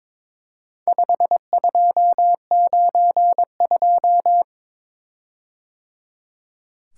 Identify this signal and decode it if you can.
Morse code, heard as a keyed tone, about 23 wpm, 700 Hz. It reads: 5292
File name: practice / 22wpm / Four Letter-Number Combinations - 22wpm